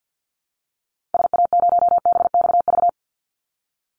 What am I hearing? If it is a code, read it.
Morse code, heard as a keyed tone, about 50 wpm, 700 Hz. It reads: SA0BXV
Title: Call Signs Repeated 3x - 50wpm